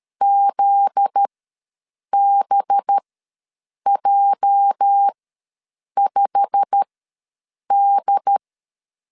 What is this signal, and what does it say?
Morse code, heard as a keyed tone, about 13 wpm, spaced as for 7 wpm, 800 Hz. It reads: ZBJ5D
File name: LEZIONI / 9 28 ws 12wpm 3354